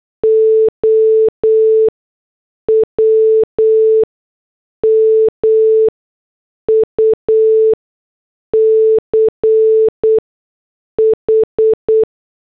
Morse code, heard as a keyed tone, about 8 wpm, spaced as for 6 wpm, 440 Hz. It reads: OW MUCH
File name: morse / headlines